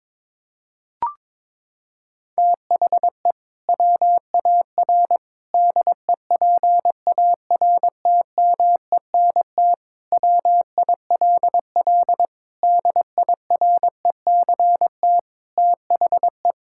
Morse code, heard as a keyed tone, about 22 wpm, 700 Hz. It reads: THE WAR DEPARTMENT WILL DIRECT THE
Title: Sentences from Top 1000 Words - Review Part 5 - 22wpm